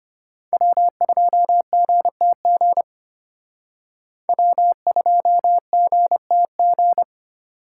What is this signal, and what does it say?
Morse code, heard as a keyed tone, about 31 wpm, 700 Hz. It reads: W2GTG W2GTG